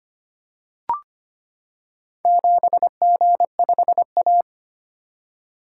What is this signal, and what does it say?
Morse code, heard as a keyed tone, about 25 wpm, 700 Hz. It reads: 7G5A
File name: practice / 25wpm / Four Letter-Number Combinations - 25wpm